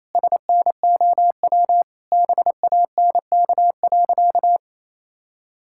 Morse code, heard as a keyed tone, about 28 wpm, 700 Hz. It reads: SNOW BANK.
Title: Sentences from Top 1600 Words 28wpm